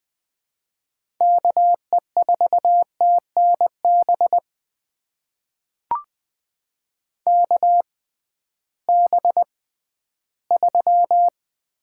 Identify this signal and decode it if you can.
Morse code, heard as a keyed tone, about 20 wpm, spaced as for 13 wpm, 700 Hz. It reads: KE4TNB KB3
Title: USA Call Signs - 6 Characters - 6x Character Spacing - 20wpm